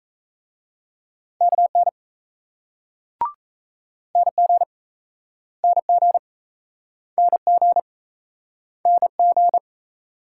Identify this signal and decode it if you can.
Morse code, heard as a keyed tone, about 40 wpm, 700 Hz. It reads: KN NG NG NG NG